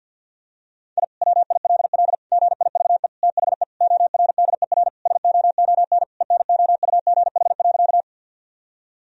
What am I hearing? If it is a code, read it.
Morse code, heard as a keyed tone, about 51 wpm, 700 Hz. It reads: I WILL GIVE THE ORDER SOON ENOUGH.
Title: Sentences from Top 1000 Words - Review Part 2 - 50wpm